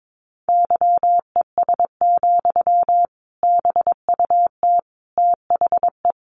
Morse code, heard as a keyed tone, about 22 wpm, 700 Hz. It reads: YES, BUT THE
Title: Sentences from Top 1200 Words 22wpm